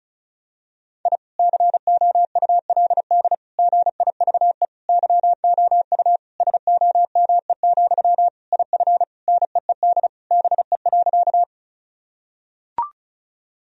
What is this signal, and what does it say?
Morse code, heard as a keyed tone, about 34 wpm, 700 Hz. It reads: I COULD GIVE YOU SOME, IF NEED BE.
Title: Sentences from Top 1000 Words - Review Part 1 - 35wpm